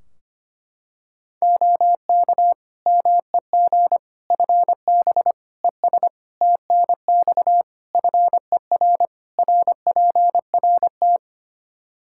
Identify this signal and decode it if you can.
Morse code, heard as a keyed tone, about 25 wpm, 700 Hz. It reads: OK MEG FB ES TNX FER RPRT
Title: Practice QSO - Exchange 2 - Info - 25wpm